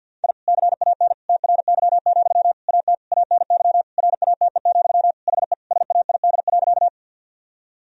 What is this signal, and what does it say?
Morse code, heard as a keyed tone, about 47 wpm, 700 Hz. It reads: I CAN TRY, AT ANY RATE, HE SAID.
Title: Sentences from Top 1000 Words - Review Part 4 - 50wpm